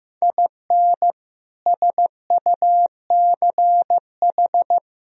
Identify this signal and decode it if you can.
Morse code, heard as a keyed tone, about 16 wpm, 700 Hz. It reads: IN SUCH
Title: Speed-Racing - Sets of 5 Words 15wpm